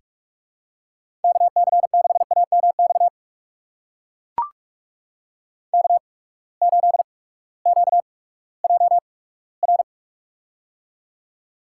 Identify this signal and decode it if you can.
Morse code, heard as a keyed tone, about 44 wpm, spaced as for 20 wpm, 700 Hz. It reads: KC6AMX K8QJR
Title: USA Call Signs - 8x Character Spacing - 45wpm